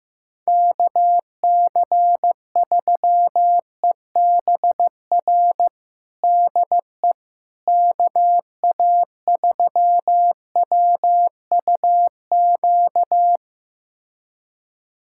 Morse code, heard as a keyed tone, about 15 wpm, 700 Hz. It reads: KC3EBR DE KA3WUQ
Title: USA Callsign DE Callsign - 6 Characters - 15wpm